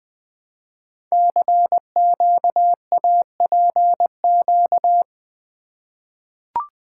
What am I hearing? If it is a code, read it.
Morse code, heard as a keyed tone, about 20 wpm, 700 Hz. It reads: CQAPQ
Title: Code Groups - 5 Letters - 20wpm